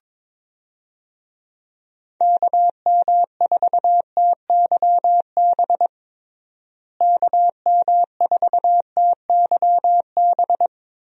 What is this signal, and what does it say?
Morse code, heard as a keyed tone, about 22 wpm, 700 Hz. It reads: KM4TYB KM4TYB
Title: USA Call Signs - 6 Characters - Repeated 2x - 22wpm